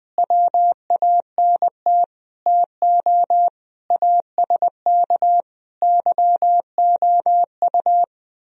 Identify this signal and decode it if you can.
Morse code, heard as a keyed tone, about 20 wpm, 700 Hz. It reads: WANT TO ASK YOU
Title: Speed-Racing - Sets of 5 Words 17wpm